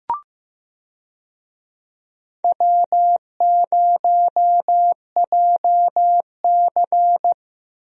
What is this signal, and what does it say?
Morse code, heard as a keyed tone, about 15 wpm, 700 Hz. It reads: W0JC
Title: USA Call Signs - 4 Characters - Repeated 3x - 15wpm